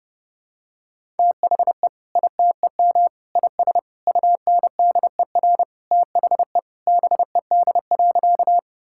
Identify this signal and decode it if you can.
Morse code, heard as a keyed tone, about 30 wpm, 700 Hz. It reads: THE ITEM IS UNDER THE BED.
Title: Sentences from Top 1000 Words - Review Part 6 - 30wpm